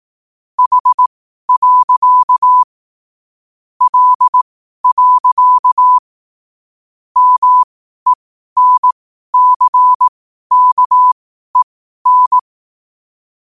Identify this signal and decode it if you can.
Morse code, heard as a keyed tone, about 18 wpm, spaced as for 12 wpm, 1000 Hz. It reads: H. L. MENCKEN
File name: qotdfiles / 12wpm